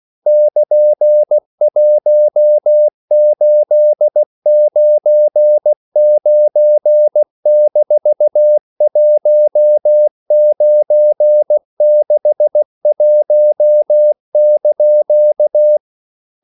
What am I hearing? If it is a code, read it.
Morse code, heard as a keyed tone, about 16 wpm, 600 Hz. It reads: <KN>1899-1961)
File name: CW / Practice / EnglishQuotations0416